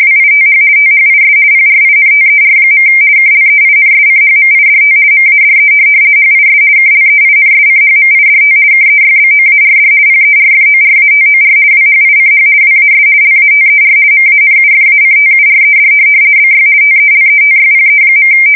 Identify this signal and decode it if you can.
RTTY, about 50 baud. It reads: WITH THIS PRINTER. KLEINSCHMIDT MADE HIS KLEINSCHMIDT KEYBOARD PERFORATOR WHICH WAS LATER MANUFACTURED BY TELETYPE.
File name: fsk170-50B / teletype-corp